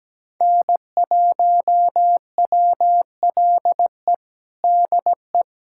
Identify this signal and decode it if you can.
Morse code, heard as a keyed tone, about 17 wpm, 700 Hz. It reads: N1WLE DE
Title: USA Callsign DE Callsign - 5 Characters - 17wpm